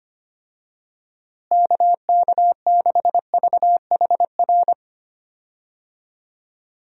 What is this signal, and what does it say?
Morse code, heard as a keyed tone, about 25 wpm, 700 Hz. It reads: KK6VHR
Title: USA Call Signs - 6 Characters - Repeated 3x - 25wpm